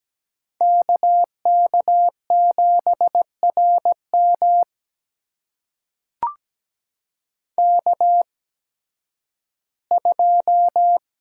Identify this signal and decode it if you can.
Morse code, heard as a keyed tone, about 17 wpm, spaced as for 11 wpm, 700 Hz. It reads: KK7RM K2
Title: USA Call Signs - 5 Characters - 8x Character Spacing - 17wpm